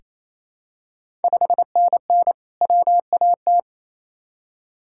Morse code, heard as a keyed tone, about 28 wpm, 700 Hz. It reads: 5NN WAT
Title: Ontario QSO Party Exchange 28wpm